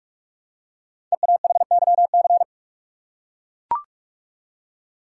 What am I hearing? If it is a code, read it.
Morse code, heard as a keyed tone, about 44 wpm, 700 Hz. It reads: EAHYC
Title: Code Groups - 5 Letters - 45wpm